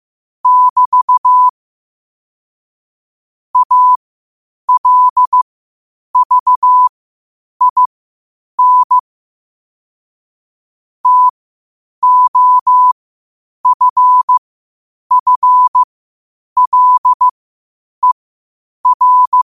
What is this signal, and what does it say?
Morse code, heard as a keyed tone, about 15 wpm, spaced as for 8 wpm, 1000 Hz. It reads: <BT> ALVIN TOFFLER
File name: 8wpm